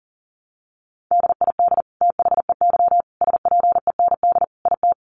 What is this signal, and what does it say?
Morse code, heard as a keyed tone, about 40 wpm, 700 Hz. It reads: DID THEY SPEND IT
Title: Sentences from Top 800 Words 40wpm